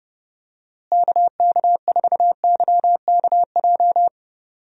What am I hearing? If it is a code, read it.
Morse code, heard as a keyed tone, about 31 wpm, 700 Hz. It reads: KK4YKJ
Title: USA Call Signs - 6 Characters - Repeated 2x - 30wpm